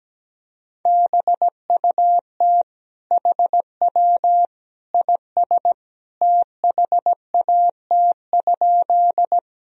Morse code, heard as a keyed tone, about 17 wpm, 700 Hz. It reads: BUT HW IS THAT?
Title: Sentences from Top 100 Words and Abbreviations 17wpm